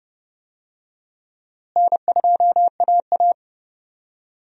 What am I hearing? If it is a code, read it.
Morse code, heard as a keyed tone, about 29 wpm, 700 Hz. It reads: N2AA
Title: USA Call Signs - 4 Characters - Repeated 3x - 30wpm